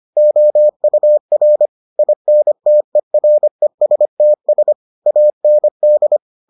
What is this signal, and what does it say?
Morse code, heard as a keyed tone, about 25 wpm, 600 Hz. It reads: OUR INTERESTS AND